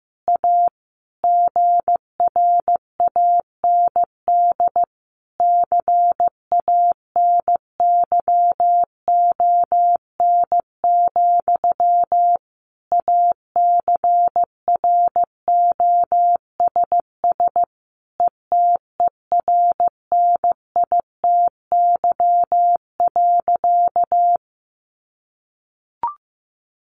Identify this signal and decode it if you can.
Morse code, heard as a keyed tone, about 15 wpm, 700 Hz. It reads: A GRAND CANYON, ACROSS ETERNITY.
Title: A Sound Of Thunder 15wpm